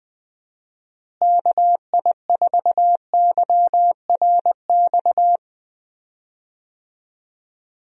Morse code, heard as a keyed tone, about 21 wpm, 700 Hz. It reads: KI4YRX
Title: USA Call Signs - 6 Characters - Repeated 2x - 20wpm